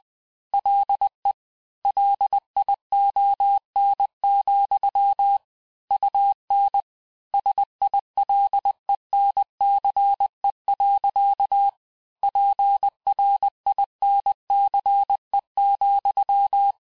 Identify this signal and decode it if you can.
Morse code, heard as a keyed tone, about 20 wpm, 800 Hz. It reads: LE LION, UN SILENCE. PRINCE,
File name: cours-du-mercredi / audios / cours-870-texte-CLAIR-vitesse-19-MPM